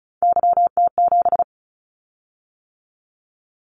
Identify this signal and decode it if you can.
Morse code, heard as a keyed tone, about 36 wpm, 700 Hz. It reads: YT7
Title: Callsign Prefixes for Top 25 DXCC - 35wpm